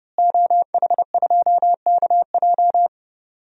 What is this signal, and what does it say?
Morse code, heard as a keyed tone, about 32 wpm, 700 Hz. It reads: OH2KJ
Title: DXCC-100 Call Signs - 5 Characters - Repeated 2x - 30wpm